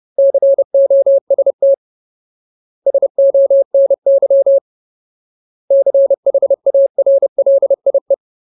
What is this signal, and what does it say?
Morse code, heard as a keyed tone, about 30 wpm, 550 Hz. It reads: COST SONY CHARLIE
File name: Ultimate Word List / 30WPM Ultimate